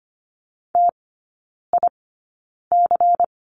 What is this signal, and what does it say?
Morse code, heard as a keyed tone, about 25 wpm, spaced as for 9 wpm, 700 Hz. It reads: TIC
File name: course / 25:10wpm / 093 Lesson 16e - Letter c - Three Characters 25wpm